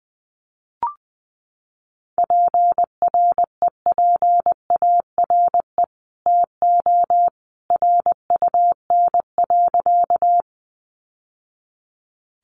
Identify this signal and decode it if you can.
Morse code, heard as a keyed tone, about 19 wpm, 700 Hz. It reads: PREPARE TO RUN.
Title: Sentences from Top 900 Words 20wpm